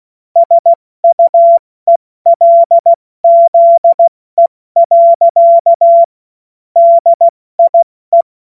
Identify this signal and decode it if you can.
Morse code, heard as a keyed tone, about 16 wpm, 680 Hz. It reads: SUELZE. DIE
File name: Morsekurs / Witze / Witz 01. November 2022 70 F80